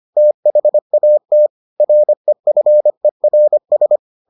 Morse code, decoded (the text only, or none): THAT REFERS